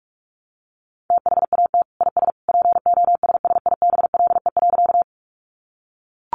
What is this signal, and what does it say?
Morse code, heard as a keyed tone, about 44 wpm, 700 Hz. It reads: THAT IS POSSIBLE.
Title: Sentences from Top 500 Words 45wpm